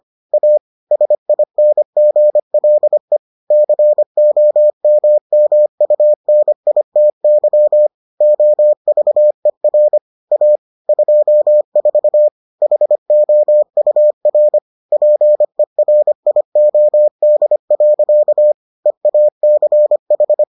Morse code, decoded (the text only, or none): A SINGLE COMMUNITY OVER A 24 HOUR PERIOD. EACH